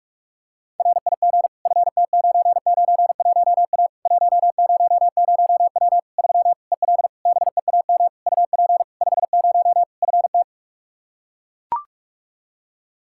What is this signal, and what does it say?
Morse code, heard as a keyed tone, about 41 wpm, 700 Hz. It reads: RIG FT991A 100W 3 EL BEAM UP 50 FT
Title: K1USN SST QSO Practice 45wpm